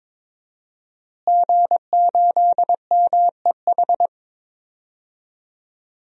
Morse code, heard as a keyed tone, about 22 wpm, 700 Hz. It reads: G8MEH
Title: DXCC-100 Call Signs - 5 Characters - Repeated 3x - 22wpm